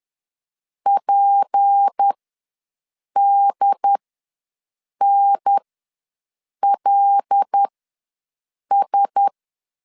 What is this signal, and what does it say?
Morse code, heard as a keyed tone, about 10 wpm, spaced as for 6 wpm, 800 Hz. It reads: PDNLS
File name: LEZIONI / 6 PKX0 / 6 P ws 10wpm 3751